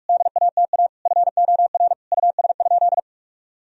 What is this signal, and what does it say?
Morse code, heard as a keyed tone, about 41 wpm, 700 Hz. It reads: DATA FOR US?
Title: Sentences from Top 1000 Words - Review Part 3 - 45wpm